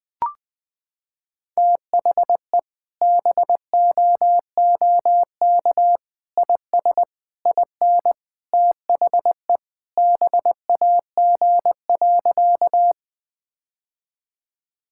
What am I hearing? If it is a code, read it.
Morse code, heard as a keyed tone, about 21 wpm, 700 Hz. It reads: THE BOOK IS IN THE BAG.
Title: Sentences with 3-Word Phrases - 20wpm